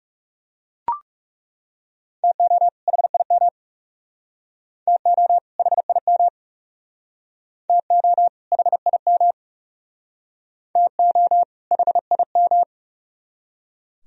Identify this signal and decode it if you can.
Morse code, heard as a keyed tone, about 37 wpm, 700 Hz. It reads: TO HIM TO HIM TO HIM TO HIM
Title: Speed-Racing - Sets of 2 Words 30wpm